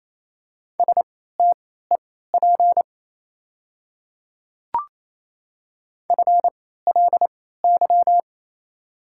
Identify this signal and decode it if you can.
Morse code, heard as a keyed tone, about 28 wpm, spaced as for 16 wpm, 690 Hz. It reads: STEP FLY
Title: 3x Character Spacing - Top 500 Words - 28wpm